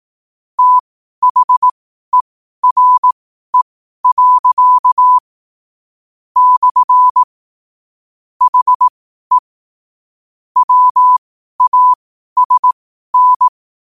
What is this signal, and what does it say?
Morse code, heard as a keyed tone, about 18 wpm, spaced as for 12 wpm, 1000 Hz. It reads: THERE. / HE WASN
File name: qotdfiles / 12wpm